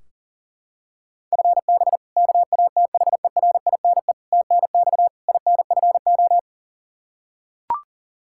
Morse code, heard as a keyed tone, about 40 wpm, 700 Hz. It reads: FB KATHERINE TNX INFO
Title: Practice QSO - Exchange 3 - More Info - 40wpm